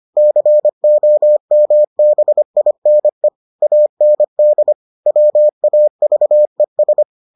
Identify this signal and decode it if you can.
Morse code, heard as a keyed tone, about 25 wpm, 600 Hz. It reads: COMBINE AND WAVES